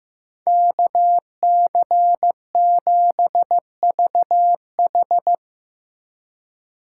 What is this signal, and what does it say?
Morse code, heard as a keyed tone, about 17 wpm, 700 Hz. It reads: KC7VH
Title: Speed Racing - USA 2x2 Call Signs 15wpm